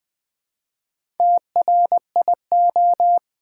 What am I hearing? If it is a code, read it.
Morse code, heard as a keyed tone, about 20 wpm, 700 Hz. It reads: TRIO
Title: Build A Word - 1 Letter At A Time - 4-Letter Words - 20wpm